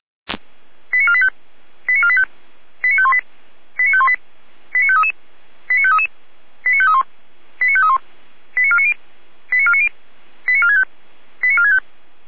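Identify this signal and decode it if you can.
Five-tone selective call (ZVEI), heard as quick runs of five tones, tones of about 70 ms. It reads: 89485 89485 89518 89518 89533 89533 89531 89531 89490 89490 89465 89465
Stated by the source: ZVEI five-tone selective call received on SDR.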